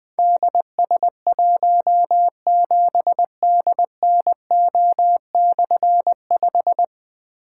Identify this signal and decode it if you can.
Morse code, heard as a keyed tone, about 20 wpm, 700 Hz. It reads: DS17DNO/5
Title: Difficult Call Signs 20wpm